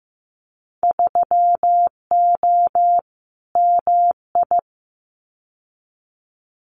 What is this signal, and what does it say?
Morse code, heard as a keyed tone, about 15 wpm, 700 Hz. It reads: 3O MI
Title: Winter Field Day Exchanges Encore 15wpm